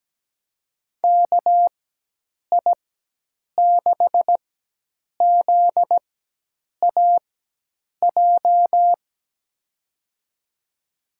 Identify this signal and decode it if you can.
Morse code, heard as a keyed tone, about 17 wpm, spaced as for 8 wpm, 700 Hz. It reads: KI6ZAJ
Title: USA Call Signs - 6 Characters - 4x Character Spacing - 17wpm